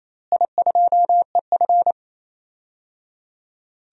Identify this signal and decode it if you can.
Morse code, heard as a keyed tone, about 28 wpm, 700 Hz. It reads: I2EF
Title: DXCC-100 Call Signs - 4 Characters - Repeated 3x - 28wpm